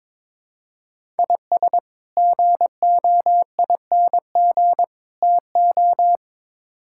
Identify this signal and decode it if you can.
Morse code, heard as a keyed tone, about 22 wpm, 700 Hz. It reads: IS GOING TO
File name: practice / 22wpm / Build A Phrase - 5-Word Phrases - 22wpm